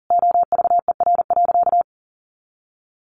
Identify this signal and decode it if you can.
Morse code, heard as a keyed tone, about 40 wpm, 700 Hz. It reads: OVER.
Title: Sentences from Top 1500 Words 40wpm